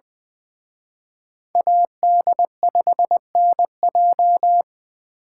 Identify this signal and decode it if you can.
Morse code, heard as a keyed tone, about 24 wpm, 700 Hz. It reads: AD5NJ
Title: Speed Racing - USA 2x2 Call Signs 20wpm